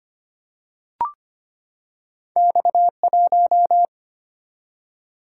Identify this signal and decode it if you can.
Morse code, heard as a keyed tone, about 25 wpm, 700 Hz. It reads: X1
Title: Single-Letter Single-Number 25wpm